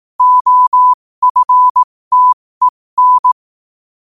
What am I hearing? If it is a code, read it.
Morse code, heard as a keyed tone, about 18 wpm, spaced as for 15 wpm, 1000 Hz. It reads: OFTEN
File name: qotdfiles / 15wpm